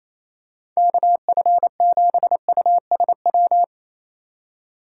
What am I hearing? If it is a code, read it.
Morse code, heard as a keyed tone, about 28 wpm, 700 Hz. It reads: KF7USW
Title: USA Call Signs - 6 Characters - 2x Character Spacing - 28wpm